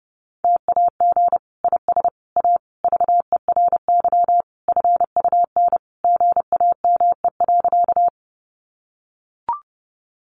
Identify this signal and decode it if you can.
Morse code, heard as a keyed tone, about 30 wpm, 700 Hz. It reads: TAG IS A VERY FUN GAME.
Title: Additional Easy Phrases 30wpm